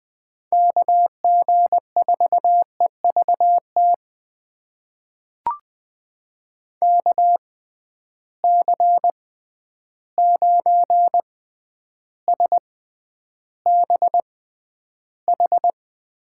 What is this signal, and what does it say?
Morse code, heard as a keyed tone, about 20 wpm, spaced as for 10 wpm, 700 Hz. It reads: KG4EVT KC9SBH
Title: USA Call Signs - 6 Characters - 6x Character Spacing - 20wpm